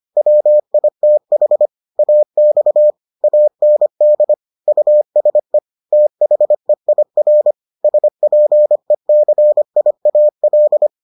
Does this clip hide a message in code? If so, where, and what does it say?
Yes: Morse code from 0.2 to 10.9 s, heard as a keyed tone, about 25 wpm, 600 Hz. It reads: WITH AX AND USE THEIR SPECIAL